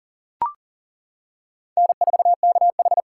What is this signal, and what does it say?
Morse code, heard as a keyed tone, about 40 wpm, 700 Hz. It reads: N4KH
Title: K1USN SST QSO Practice - 40wpm